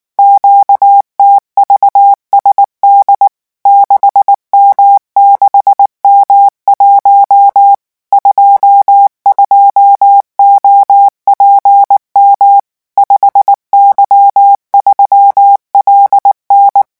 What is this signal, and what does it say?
Morse code, heard as a keyed tone, about 19 wpm, 800 Hz. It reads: QTVSD 6M6M1 22OPM 5Y3LN
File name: morse mp3 / revision10-vitesse 18 mots